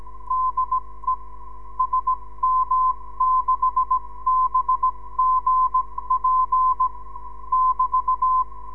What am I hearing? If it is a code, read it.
Morse code, heard as a keyed tone, about 17 wpm, spaced as for 15 wpm, 1030 Hz. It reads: DE SM6BGP <BT>
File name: ndb zvok / sm6bgp